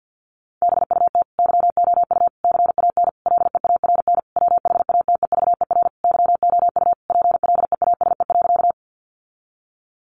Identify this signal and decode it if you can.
Morse code, heard as a keyed tone, about 50 wpm, 700 Hz. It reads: BUT YOU CAN LEARN WHATEVER YOU PLEASE.
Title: Sentences from Top 1500 Words 50wpm